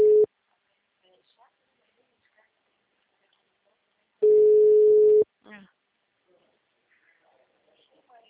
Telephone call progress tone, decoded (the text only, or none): ringback tone